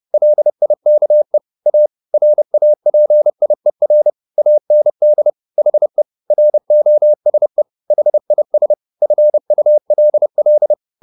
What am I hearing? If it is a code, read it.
Morse code, heard as a keyed tone, about 30 wpm, 600 Hz. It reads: LIKE A RAPIER AND HE ROSE HIS FULL
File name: the old man and the sea / the old man and the sea 1 30wpm